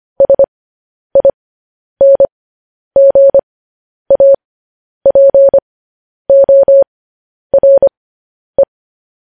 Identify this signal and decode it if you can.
Morse code, heard as a keyed tone, about 25 wpm, spaced as for 10 wpm, 550 Hz. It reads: SINGAPORE
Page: news headlines in morse code at 10 wpm